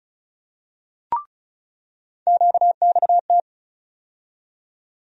Morse code, heard as a keyed tone, about 36 wpm, 700 Hz. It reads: QXT
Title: USA Call Sign Postfixes - 3 Letters 35wpm